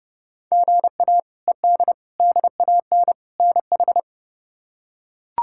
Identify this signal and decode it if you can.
Morse code, heard as a keyed tone, about 31 wpm, 700 Hz. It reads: GA ED DAN NH
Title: K1USN SST QSO Practice - 30wpm